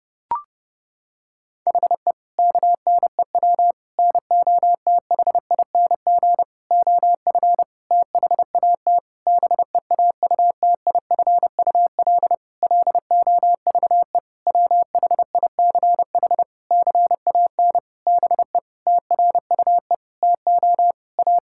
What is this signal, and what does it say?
Morse code, heard as a keyed tone, about 30 wpm, 700 Hz. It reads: HE KNEW NOTHING OF THAT BEAUTIFUL LOVE WHICH CAN BE TRUE TO A
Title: Sentences from Top 1600 Words 30wpm